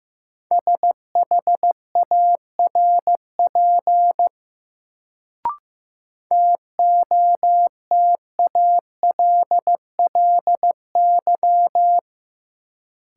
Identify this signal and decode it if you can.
Morse code, heard as a keyed tone, about 15 wpm, 700 Hz. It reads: SHARP TOTALLY
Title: Sentences from Top 1600 Words - New Word Drill Alt - 15wpm